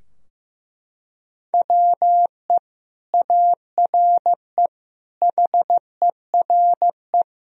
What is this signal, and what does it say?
Morse code, heard as a keyed tone, about 15 wpm, 700 Hz. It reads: WE ARE HERE
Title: Sentences from Top 1000 Words - Review Part 4 - 15wpm